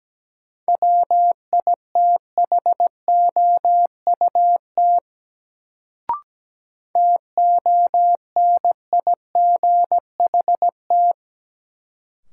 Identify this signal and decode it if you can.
Morse code, heard as a keyed tone, about 17 wpm, 700 Hz. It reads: WITHOUT TONIGHT
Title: Sentences from Top 600 Words - New Word Drill - 17wpm